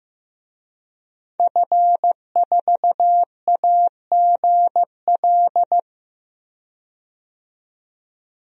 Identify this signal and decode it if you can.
Morse code, heard as a keyed tone, about 15 wpm, 700 Hz. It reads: F4AGL
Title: DXCC-100 Call Signs - 5 Characters - 15wpm